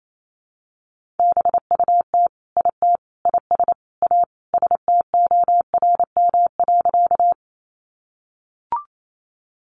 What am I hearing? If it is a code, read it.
Morse code, heard as a keyed tone, about 28 wpm, 700 Hz. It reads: BUT IT IS A STORM.
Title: Sentences from Top 1500 Words 28wpm